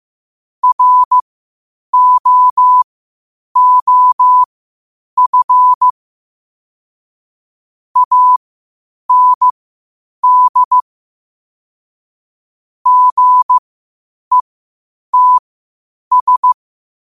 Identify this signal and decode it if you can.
Morse code, heard as a keyed tone, about 15 wpm, spaced as for 8 wpm, 1000 Hz. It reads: ROOF AND GETS
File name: qotdfiles / 8wpm